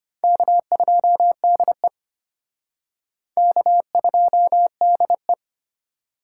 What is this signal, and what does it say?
Morse code, heard as a keyed tone, about 33 wpm, 700 Hz. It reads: K2DE K2DE